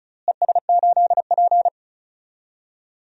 Morse code, heard as a keyed tone, about 36 wpm, 700 Hz. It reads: ES8P
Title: DXCC-100 Call Signs - 4 Characters - Repeated 3x - 35wpm